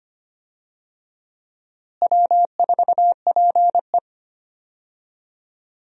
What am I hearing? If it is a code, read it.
Morse code, heard as a keyed tone, about 25 wpm, 700 Hz. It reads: W4PE
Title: USA Call Signs - 4 Characters - Repeated 3x - 25wpm